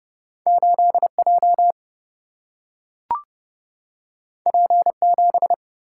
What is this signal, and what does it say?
Morse code, heard as a keyed tone, about 29 wpm, 700 Hz. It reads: 8J P7
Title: Single-Letter Single-Number 30wpm